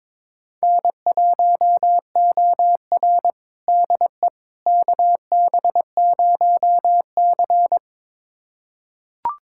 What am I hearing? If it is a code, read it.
Morse code, heard as a keyed tone, about 22 wpm, 700 Hz. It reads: N1OR DE KB0C
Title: USA Callsign DE Callsign - 4 Characters - 22wpm